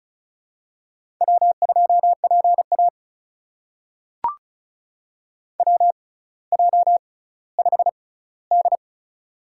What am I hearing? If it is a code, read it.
Morse code, heard as a keyed tone, about 36 wpm, spaced as for 18 wpm, 700 Hz. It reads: W2PA WJ5D
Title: USA Call Signs - 4 Characters - 6x Character Spacing - 35wpm